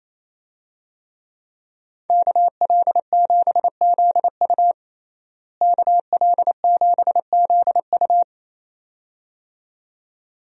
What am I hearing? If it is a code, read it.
Morse code, heard as a keyed tone, about 28 wpm, 700 Hz. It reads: KL7ZU KL7ZU
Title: USA Call Signs - 5 Characters - Repeated 2x - 28wpm